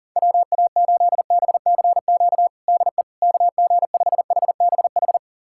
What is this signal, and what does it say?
Morse code, heard as a keyed tone, about 37 wpm, 700 Hz. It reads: WA8BCQ DE KG5HBH
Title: USA Callsign DE Callsign - 6 Characters - 40wpm